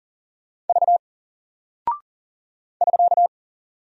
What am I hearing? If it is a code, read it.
Morse code, heard as a keyed tone, about 40 wpm, 700 Hz. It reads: V <SK>
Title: Single Character (Letter:Number:Punctuation:Prosign) Alt 40wpm